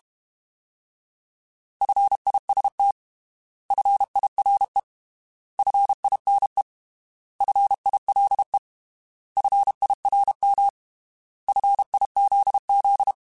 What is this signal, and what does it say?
Morse code, heard as a keyed tone, about 32 wpm, 800 Hz. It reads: FIST FIRE FINE FILE FIRM FIZZ
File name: Alphabetical / Alphabetical mp3s / Fs @30